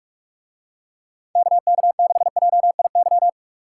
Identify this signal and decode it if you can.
Morse code, heard as a keyed tone, about 44 wpm, 700 Hz. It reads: KK6JIY